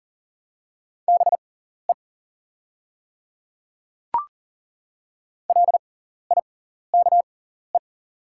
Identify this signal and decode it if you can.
Morse code, heard as a keyed tone, about 40 wpm, spaced as for 14 wpm, 690 Hz. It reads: BE LIKE
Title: 6x Character Spacing - Top 100 Words - 40wpm